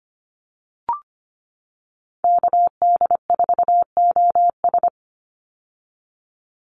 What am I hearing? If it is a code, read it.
Morse code, heard as a keyed tone, about 25 wpm, 700 Hz. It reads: KD4OS
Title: Call Signs - 25wpm